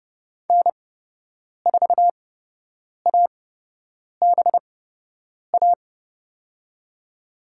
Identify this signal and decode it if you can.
Morse code, heard as a keyed tone, about 30 wpm, spaced as for 8 wpm, 700 Hz. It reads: N4ABA
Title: USA Call Signs - 8x Character Spacing - 30wpm